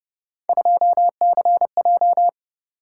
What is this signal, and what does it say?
Morse code, heard as a keyed tone, about 32 wpm, 700 Hz. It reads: 2CJ